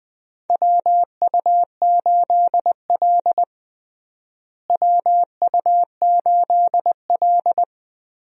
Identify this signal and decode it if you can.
Morse code, heard as a keyed tone, about 20 wpm, 700 Hz. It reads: WU8L WU8L
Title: USA Call Signs - 4 Characters - Repeated 3x - 20wpm